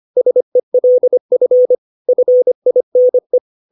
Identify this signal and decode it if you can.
Morse code, heard as a keyed tone, about 25 wpm, 500 Hz. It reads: SELF FINE
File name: 632 words / 632 Words 25WPM